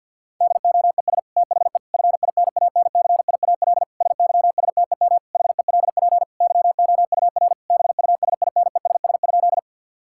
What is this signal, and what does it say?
Morse code, heard as a keyed tone, about 50 wpm, 700 Hz. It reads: DOES THE FINANCIAL SYSTEM HELP YOUR BUSINESS?
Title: Sentences from Top 1000 Words - Review Part 5 - 50wpm